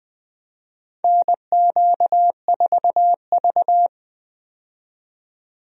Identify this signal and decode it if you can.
Morse code, heard as a keyed tone, about 20 wpm, 700 Hz. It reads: NQ4V
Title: USA Call Signs - 4 Characters - 20wpm